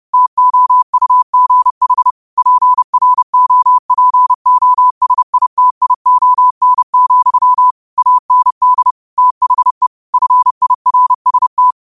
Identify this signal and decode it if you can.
Morse code, heard as a keyed tone, about 30 wpm, 1000 Hz. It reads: TOUGH PROPOSITION, AND THE FIRST